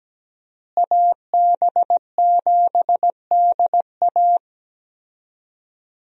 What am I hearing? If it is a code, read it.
Morse code, heard as a keyed tone, about 17 wpm, 700 Hz. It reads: AB7DA
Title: USA Call Signs - 5 Characters - Repeated 3x - 17wpm